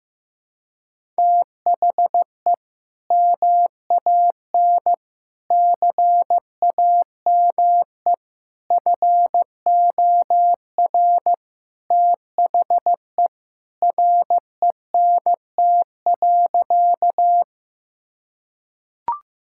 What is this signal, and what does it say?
Morse code, heard as a keyed tone, about 15 wpm, 700 Hz. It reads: THE MAN CAME FOR THE RENT.
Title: Sentences from Top 1000 Words - Review Part 5 - 15wpm